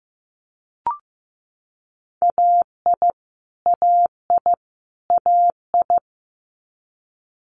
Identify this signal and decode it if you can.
Morse code, heard as a keyed tone, about 15 wpm, 700 Hz. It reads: AI AI AI
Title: Multi-letter Phonemes Repeated 3x 15wpm